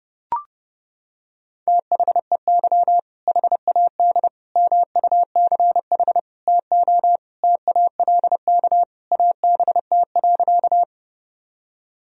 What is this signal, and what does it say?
Morse code, heard as a keyed tone, about 30 wpm, 700 Hz. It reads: THEY HAD MUCH TO TALK ABT.
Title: Sentences from Top 400 Words and Abbreviations 30wpm